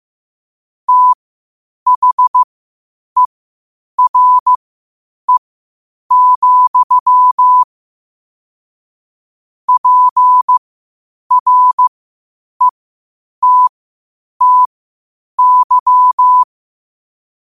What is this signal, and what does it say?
Morse code, heard as a keyed tone, about 15 wpm, spaced as for 8 wpm, 1000 Hz. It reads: THERE, PRETTY
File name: qotdfiles / 8wpm